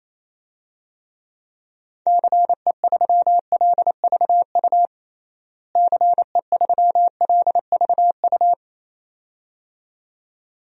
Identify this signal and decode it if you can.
Morse code, heard as a keyed tone, about 27 wpm, 700 Hz. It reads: CE3LVU CE3LVU
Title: DXCC-100 Call Signs - 6 Characters - Repeated 2x - 28wpm